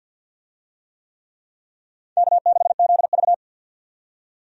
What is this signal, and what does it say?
Morse code, heard as a keyed tone, about 52 wpm, 700 Hz. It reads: K6ZV
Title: USA Call Signs - 4 Characters - 50wpm